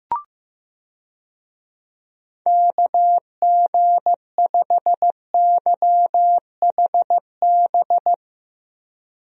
Morse code, heard as a keyed tone, about 15 wpm, 700 Hz. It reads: KG5YHB
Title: USA Call Signs - 6 Characters - Repeated 2x - 15wpm